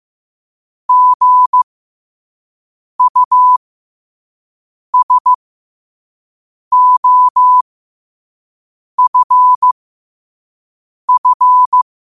Morse code, heard as a keyed tone, about 15 wpm, spaced as for 5 wpm, 1000 Hz. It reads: GUSOFF
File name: qotdfiles / 5wpm